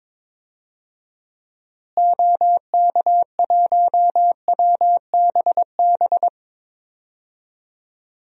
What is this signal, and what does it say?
Morse code, heard as a keyed tone, about 22 wpm, 700 Hz. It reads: OK1WBB
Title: DXCC-100 Call Signs - 6 Characters - Repeated 2x - 22wpm